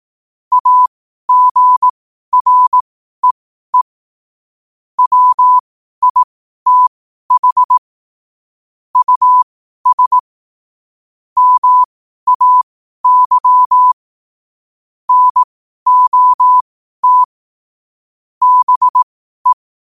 Morse code, heard as a keyed tone, about 18 wpm, spaced as for 12 wpm, 1000 Hz. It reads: AGREE WITH US MAY NOT BE
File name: qotdfiles / 12wpm